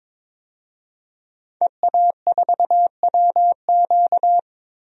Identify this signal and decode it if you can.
Morse code, heard as a keyed tone, about 22 wpm, 700 Hz. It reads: EA4WQ
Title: DXCC-100 Call Signs - 5 Characters - 22wpm